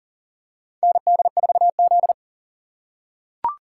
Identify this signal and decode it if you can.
Morse code, heard as a keyed tone, about 40 wpm, 700 Hz. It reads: ND4Z
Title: USA Call Signs - 4x Character Spacing - 40wpm